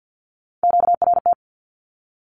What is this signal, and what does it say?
Morse code, heard as a keyed tone, about 48 wpm, 700 Hz. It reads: QRT
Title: Q-Codes 50wpm